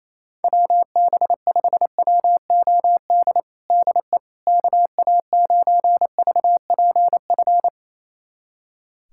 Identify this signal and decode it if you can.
Morse code, heard as a keyed tone, about 28 wpm, 700 Hz. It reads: WB5WOD DE KA9VPF
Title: USA Callsign DE Callsign - 6 Characters - 28wpm